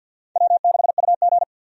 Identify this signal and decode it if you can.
Morse code, heard as a keyed tone, about 52 wpm, 700 Hz. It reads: W6UG
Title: USA Call Signs - 4 Characters - 50wpm